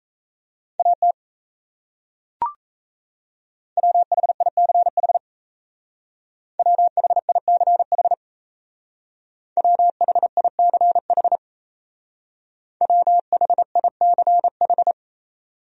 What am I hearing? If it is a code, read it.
Morse code, heard as a keyed tone, about 37 wpm, 700 Hz. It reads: AT WHICH WHICH WHICH WHICH